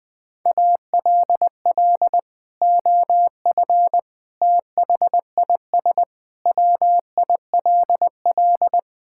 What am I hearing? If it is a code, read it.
Morse code, heard as a keyed tone, about 19 wpm, 700 Hz. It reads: ALL OF THIS WILL
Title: Sentences from Top 1000 Words - Review Part 6 - 20wpm